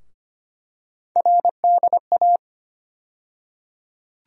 Morse code, heard as a keyed tone, about 25 wpm, 700 Hz. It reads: RDA
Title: USA Call Sign Postfixes - 3 Letters 25wpm